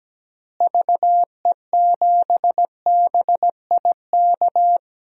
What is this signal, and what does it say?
Morse code, heard as a keyed tone, about 17 wpm, 700 Hz. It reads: VE7BIK
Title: DXCC-100 Call Signs - 6 Characters - Repeated 3x - 17wpm